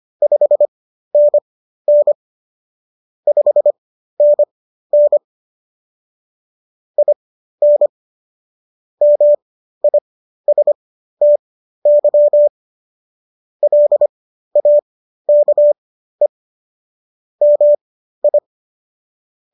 Morse code, heard as a keyed tone, about 25 wpm, spaced as for 13 wpm, 590 Hz. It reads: 5NN 5NN IN MISTY LAKE MI